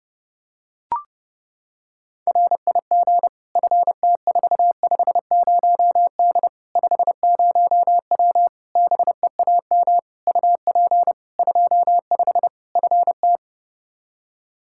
Morse code, heard as a keyed tone, about 32 wpm, 700 Hz. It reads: RIG FT450D 50W BEAM UP 25 FT
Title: Practice QSO - Exchange 2 - Info - 30wpm